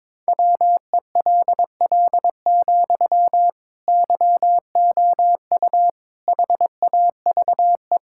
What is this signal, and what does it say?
Morse code, heard as a keyed tone, about 21 wpm, 700 Hz. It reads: WELL, YOU HAVE
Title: Sentences from Top 1000 Words - Review Part 6 - 22wpm